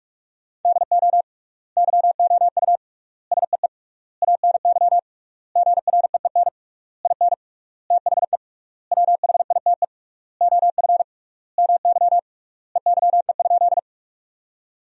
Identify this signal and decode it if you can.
Morse code, heard as a keyed tone, about 46 wpm, 700 Hz. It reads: DO YOU SEE ANY GREEN IN THE WHITE OF MY EYE?
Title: 3x Word-Spacing - Sentences from Top 500 Words - 45wpm